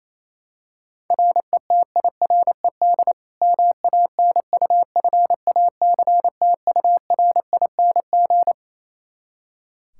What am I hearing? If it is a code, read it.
Morse code, heard as a keyed tone, about 28 wpm, 700 Hz. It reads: RETIRED MANUFACTURING
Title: Practice QSO - Exchange 3 - More Info - 28wpm